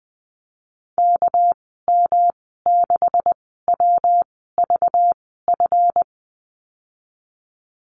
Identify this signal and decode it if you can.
Morse code, heard as a keyed tone, about 20 wpm, spaced as for 15 wpm, 700 Hz. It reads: KM6WVF
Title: 200 Lesson 32g - Letter k - Call Signs 20wpm